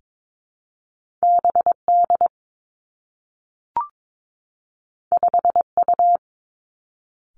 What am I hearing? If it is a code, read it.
Morse code, heard as a keyed tone, about 22 wpm, 700 Hz. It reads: B D 5 U
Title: Two Challenging Letter-Number Combinations 22wpm